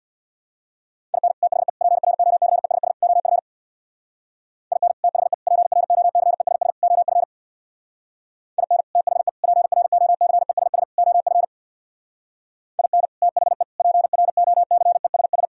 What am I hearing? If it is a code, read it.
Morse code, heard as a keyed tone, about 64 wpm, 700 Hz. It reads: IN THE PROCESS OF IN THE PROCESS OF IN THE PROCESS OF IN THE PROCESS